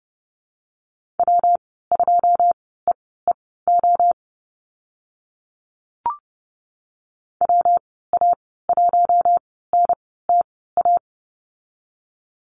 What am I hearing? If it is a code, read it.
Morse code, heard as a keyed tone, about 29 wpm, spaced as for 17 wpm, 700 Hz. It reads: W2EEO WA1NTA
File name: Lesson 19g - Letter w - Call Signs 30wpm